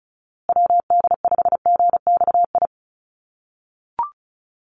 Morse code, heard as a keyed tone, about 34 wpm, 700 Hz. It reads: WD5GXI
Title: USA Call Signs Repeated 2x 35wpm